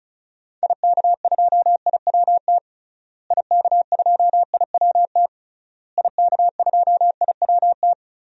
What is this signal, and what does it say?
Morse code, heard as a keyed tone, about 36 wpm, 700 Hz. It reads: IK2IWT IK2IWT IK2IWT
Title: DXCC-100 Call Signs - 6 Characters - Repeated 3x - 35wpm